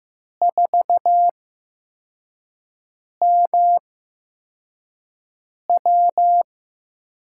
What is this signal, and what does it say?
Morse code, heard as a keyed tone, about 15 wpm, spaced as for 4 wpm, 700 Hz. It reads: 4MW
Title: USA Call Signs - 4 Characters - 8x Character Spacing - 15wpm